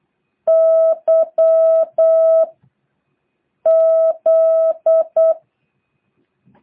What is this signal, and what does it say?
Morse code, heard as a keyed tone, about 8 wpm, 650 Hz. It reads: Y Z
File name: jh8cbh / 2023cw